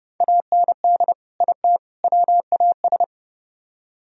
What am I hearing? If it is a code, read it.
Morse code, heard as a keyed tone, about 30 wpm, 700 Hz. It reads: AND IT WAS